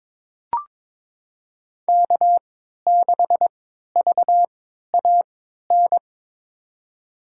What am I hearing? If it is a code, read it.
Morse code, heard as a keyed tone, about 22 wpm, spaced as for 12 wpm, 700 Hz. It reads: K6VAN
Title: USA Call Signs - 5 Characters - 3x Character Spacing - 22wpm